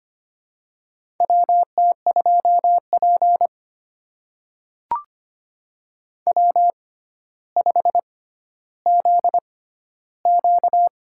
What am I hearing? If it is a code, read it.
Morse code, heard as a keyed tone, about 25 wpm, spaced as for 13 wpm, 700 Hz. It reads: WT2P W5ZQ
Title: USA Call Signs - 4 Characters - 6x Character Spacing - 25wpm